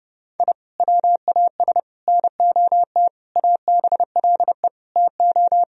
Morse code, heard as a keyed tone, about 30 wpm, 700 Hz. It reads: I WAS NOT ABLE TO
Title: Sentences from Top 1000 Words - Review Part 4 - 30wpm